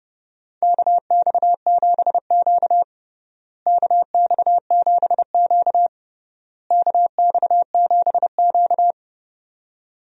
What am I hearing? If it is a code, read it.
Morse code, heard as a keyed tone, about 31 wpm, 700 Hz. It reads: KX7Q KX7Q KX7Q
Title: USA Call Signs - 4 Characters - Repeated 3x - 30wpm